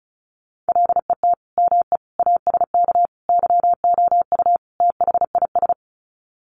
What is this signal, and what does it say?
Morse code, heard as a keyed tone, about 34 wpm, 700 Hz. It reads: LET ME ASK YOU THIS
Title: Sets of 5 Words 35wpm